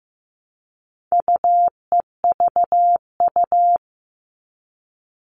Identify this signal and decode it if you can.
Morse code, heard as a keyed tone, about 15 wpm, 700 Hz. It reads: UEVU
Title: All Cut Numbers - 4-Digit 15wpm